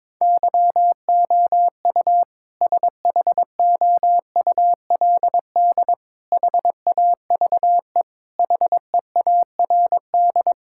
Morse code, heard as a keyed tone, about 22 wpm, 700 Hz. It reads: YOU SHOULD HAVE HEARD
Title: Sentences from Top 1600 Words 22wpm